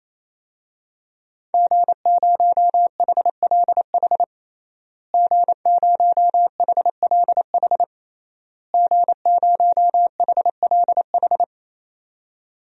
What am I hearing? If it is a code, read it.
Morse code, heard as a keyed tone, about 28 wpm, 700 Hz. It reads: G0HLH G0HLH G0HLH
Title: DXCC-100 Call Signs - 5 Characters - Repeated 3x - 28wpm